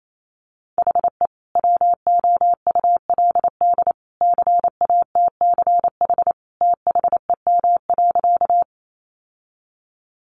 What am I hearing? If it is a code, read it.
Morse code, heard as a keyed tone, about 28 wpm, 700 Hz. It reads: HE WOULD CATCH THEM.
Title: Sentences from Top 1500 Words 28wpm